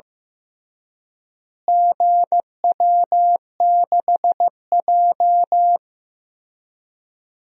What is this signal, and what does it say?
Morse code, heard as a keyed tone, about 16 wpm, 700 Hz. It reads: GW6J
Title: DXCC-100 Call Signs - 4 Characters - Repeated 3x - 15wpm